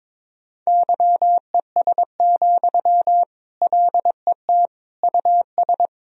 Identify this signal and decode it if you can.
Morse code, heard as a keyed tone, about 21 wpm, 700 Hz. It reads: YES, LET US
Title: Sentences from Top 1000 Words - Review Part 1 - 22wpm